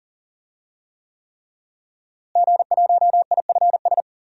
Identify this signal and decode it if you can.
Morse code, heard as a keyed tone, about 40 wpm, 700 Hz. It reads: G1IFS
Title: DXCC-100 Call Signs - 5 Characters - Repeated 3x - 40wpm